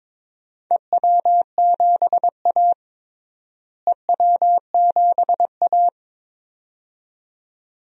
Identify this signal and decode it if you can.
Morse code, heard as a keyed tone, about 22 wpm, 700 Hz. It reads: EW7A EW7A
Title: DXCC-100 Call Signs - 4 Characters - Repeated 2x - 22wpm